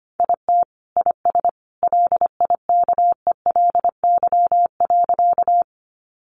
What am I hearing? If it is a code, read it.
Morse code, heard as a keyed tone, about 25 wpm, 700 Hz. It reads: IT IS LIKELY.
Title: Sentences from Top 600 Words 25wpm